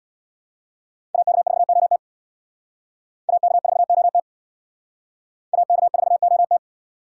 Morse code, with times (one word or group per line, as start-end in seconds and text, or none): 1.1-2.0 AK4OT
3.3-4.2 AK4OT
5.5-6.6 AK4OT